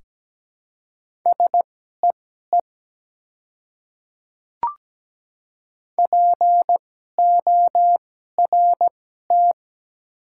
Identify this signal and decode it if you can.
Morse code, heard as a keyed tone, about 17 wpm, spaced as for 12 wpm, 700 Hz. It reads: SEE PORT
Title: 2x Character Spacing - Top 300 Words - 17wpm